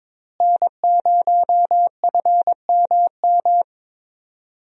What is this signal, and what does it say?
Morse code, heard as a keyed tone, about 22 wpm, 700 Hz. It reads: N0FMM
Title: USA Call Signs - 5 Characters - Repeated 3x - 22wpm